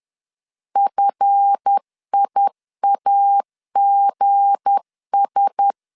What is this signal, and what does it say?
Morse code, heard as a keyed tone, about 10 wpm, 800 Hz. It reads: FIAGS
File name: LEZIONI / 8 YJ45 / 8 Y 10wpm 2615